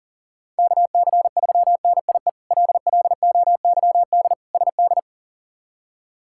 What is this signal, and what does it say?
Morse code, heard as a keyed tone, about 40 wpm, 700 Hz. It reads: KC3NIE LLOYD SD
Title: K1USN SST QSO Practice - 40wpm